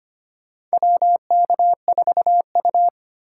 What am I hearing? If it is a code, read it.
Morse code, heard as a keyed tone, about 25 wpm, 700 Hz. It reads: WK4U
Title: USA Call Signs - 3x Character Spacing - 25wpm